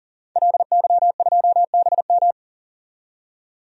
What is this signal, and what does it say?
Morse code, heard as a keyed tone, about 37 wpm, 700 Hz. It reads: LY2BM
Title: DXCC-100 Call Signs - 5 Characters - 40wpm